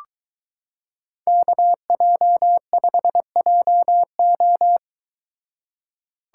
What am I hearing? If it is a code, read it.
Morse code, heard as a keyed tone, about 22 wpm, 700 Hz. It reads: KJ5JO